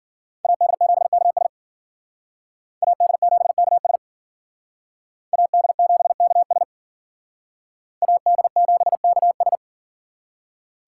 Morse code, heard as a keyed tone, about 53 wpm, 700 Hz. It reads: AD7KS AD7KS AD7KS AD7KS